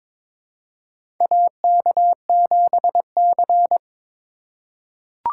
Morse code, heard as a keyed tone, about 22 wpm, 700 Hz. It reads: AK7C
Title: USA Call Signs - 4 Characters - 4x Character Spacing - 22wpm